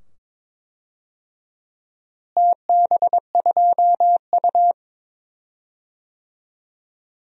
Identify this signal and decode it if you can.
Morse code, heard as a keyed tone, about 22 wpm, 700 Hz. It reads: TB2U
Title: DXCC-100 Call Signs - 4 Characters - Repeated 2x - 22wpm